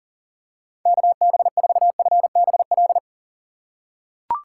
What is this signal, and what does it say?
Morse code, heard as a keyed tone, about 37 wpm, 700 Hz. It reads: KB4FBL
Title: USA Call Signs - 6 Characters - 3x Character Spacing - 40wpm